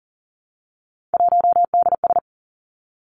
Morse code, heard as a keyed tone, about 40 wpm, 700 Hz. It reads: 1DS
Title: Three Challenging Letter-Number Combinations 40wpm